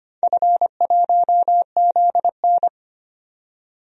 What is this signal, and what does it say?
Morse code, heard as a keyed tone, about 25 wpm, 700 Hz. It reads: F1ZN
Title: DXCC-100 Call Signs - 4 Characters - 25wpm